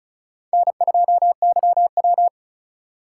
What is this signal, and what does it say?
Morse code, heard as a keyed tone, about 34 wpm, 700 Hz. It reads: N2YW
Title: USA Call Signs - 4 Characters - 3x Character Spacing - 35wpm